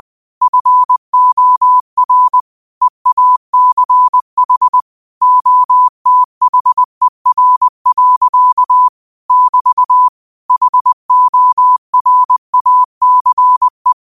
Morse code, heard as a keyed tone, about 20 wpm, 1000 Hz. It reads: FOR EACH OTHER. <BT> HORACE